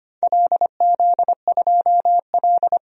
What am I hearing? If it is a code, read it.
Morse code, heard as a keyed tone, about 25 wpm, 700 Hz. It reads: LZ2L
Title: DXCC-100 Call Signs - 4 Characters - 25wpm